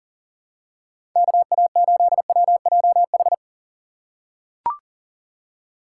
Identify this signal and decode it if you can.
Morse code, heard as a keyed tone, about 40 wpm, 700 Hz. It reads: KA8WJH
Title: USA Call Signs - 6 Characters - 2x Character Spacing - 40wpm